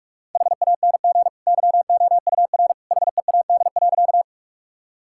Morse code, heard as a keyed tone, about 45 wpm, 700 Hz. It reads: HANG YOUR HEAD.
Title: Sentences from Top 1600 Words 45wpm